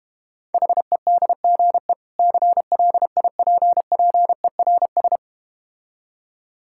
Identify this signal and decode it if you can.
Morse code, heard as a keyed tone, about 31 wpm, 700 Hz. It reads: HEDGE CLIPPERS
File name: practice / 25wpm / Difficult Words - Speed-Racing with Hints - 25wpm